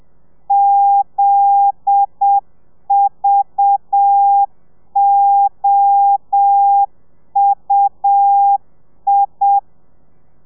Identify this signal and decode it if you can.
Morse code, heard as a keyed tone, about 7 wpm, 800 Hz. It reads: ZVOUI
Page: Lezione 1: audio a 30 CPM